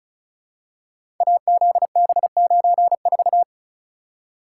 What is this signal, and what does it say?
Morse code, heard as a keyed tone, about 34 wpm, 700 Hz. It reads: AZB94
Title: Code Groups - 5 Letter:Numbers - 35wpm